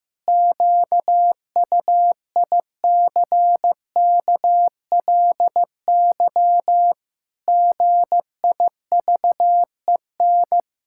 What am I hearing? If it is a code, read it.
Morse code, heard as a keyed tone, about 15 wpm, 700 Hz. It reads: QUICKLY GIVEN
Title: Pangram Challenge v2 - 15wpm